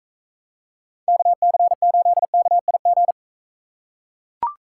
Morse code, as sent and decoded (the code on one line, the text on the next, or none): -.- -.-. ---.. -.- .. --.
KC8KIG